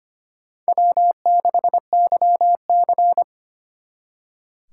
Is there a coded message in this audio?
Yes: Morse code from 0.7 to 3.2 s, heard as a keyed tone, about 32 wpm, 700 Hz. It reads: W6YC